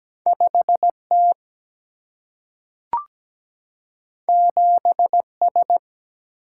Morse code, heard as a keyed tone, about 17 wpm, 700 Hz. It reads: 5 T 7 S
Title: Single-Letter Single-Number 17wpm